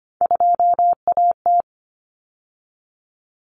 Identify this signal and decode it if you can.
Morse code, heard as a keyed tone, about 25 wpm, 700 Hz. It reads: 2AT
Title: Common Cut Numbers - 3-Digit 25wpm